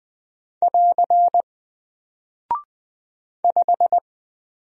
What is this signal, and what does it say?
Morse code, heard as a keyed tone, about 20 wpm, 700 Hz. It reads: <AR> 5
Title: Single Character (Letter:Number:Punctuation:Prosign) Alt 20wpm